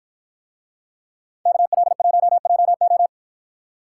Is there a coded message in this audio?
Yes: Morse code from 1.5 to 3.1 s, heard as a keyed tone, about 48 wpm, 700 Hz. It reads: KL1JO